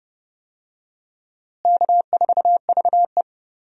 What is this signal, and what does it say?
Morse code, heard as a keyed tone, about 32 wpm, 700 Hz. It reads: K4VE
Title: USA Call Signs - 4 Characters - Repeated 2x - 30wpm